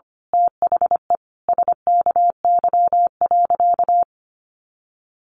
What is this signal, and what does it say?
Morse code, heard as a keyed tone, about 25 wpm, 700 Hz. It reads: THE SKY.
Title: A Sound Of Thunder 25wpm